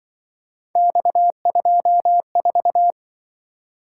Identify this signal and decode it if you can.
Morse code, heard as a keyed tone, about 24 wpm, 700 Hz. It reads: X24
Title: Speed-Racing - Three Letter-Number Combinations - 20wpm